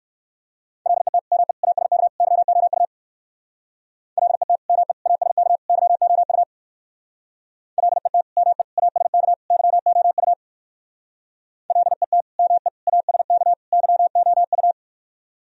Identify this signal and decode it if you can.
Morse code, heard as a keyed tone, about 59 wpm, 700 Hz. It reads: LET ME ASK YOU LET ME ASK YOU LET ME ASK YOU LET ME ASK YOU